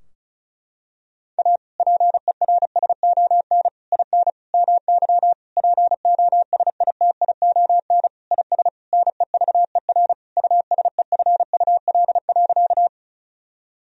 Morse code, heard as a keyed tone, about 36 wpm, 700 Hz. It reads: A PERSON IN MY POSITION IS NEVER USEFUL.
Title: Sentences from Top 1000 Words - Review Part 4 - 35wpm